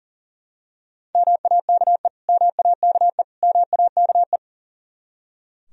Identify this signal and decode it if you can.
Morse code, heard as a keyed tone, about 39 wpm, 700 Hz. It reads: MAKE MAKE MAKE
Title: Top 100 Words Repeated 40wpm